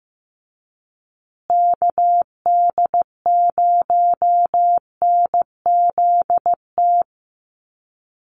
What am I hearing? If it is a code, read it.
Morse code, heard as a keyed tone, about 15 wpm, 700 Hz. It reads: KD0NZT
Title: USA Call Signs Repeated 3x 15wpm